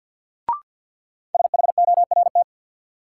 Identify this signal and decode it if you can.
Morse code, heard as a keyed tone, about 47 wpm, 700 Hz. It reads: SHORT
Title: Sets of 5 Words Encore - New Word Drill - 50wpm